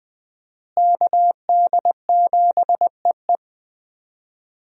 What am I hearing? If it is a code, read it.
Morse code, heard as a keyed tone, about 20 wpm, 700 Hz. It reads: KD7EE
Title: USA Call Signs - 5 Characters - Repeated 2x - 20wpm